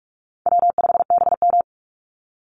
Morse code, heard as a keyed tone, about 44 wpm, 700 Hz. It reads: W5BM
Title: Easy Call Signs Repeated 3x - 45wpm